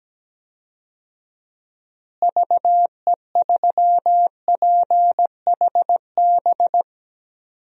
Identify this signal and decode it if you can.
Morse code, heard as a keyed tone, about 17 wpm, 700 Hz. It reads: VE3PHB
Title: DXCC-100 Call Signs - 6 Characters - Repeated 3x - 17wpm